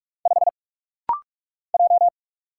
Morse code, heard as a keyed tone, about 44 wpm, 700 Hz. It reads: 5 J
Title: Single Letter-Number - Rapid-Fire 45wpm